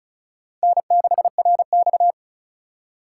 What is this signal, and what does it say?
Morse code, heard as a keyed tone, about 42 wpm, 700 Hz. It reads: N6RX
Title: Speed Racing - USA Call Signs 35wpm